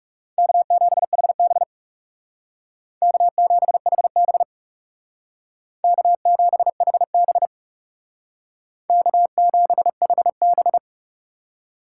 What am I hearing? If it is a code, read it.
Morse code, heard as a keyed tone, about 38 wpm, 700 Hz. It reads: K7HB K7HB K7HB K7HB